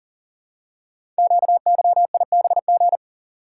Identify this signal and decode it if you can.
Morse code, heard as a keyed tone, about 39 wpm, 700 Hz. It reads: QYIBG
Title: Code Groups - 5 Letters - 40wpm